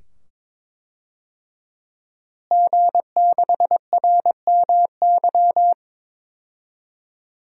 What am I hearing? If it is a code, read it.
Morse code, heard as a keyed tone, about 22 wpm, 700 Hz. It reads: G6RMY
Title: DXCC-100 Call Signs - 5 Characters - Repeated 2x - 22wpm